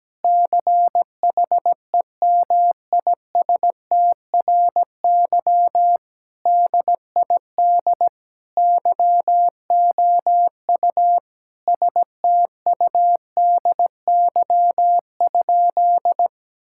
Morse code, heard as a keyed tone, about 17 wpm, 700 Hz. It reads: CHEMISTRY DID YOU STUDY?
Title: Sentences from Top 1000 Words - Review Part 6 - 17wpm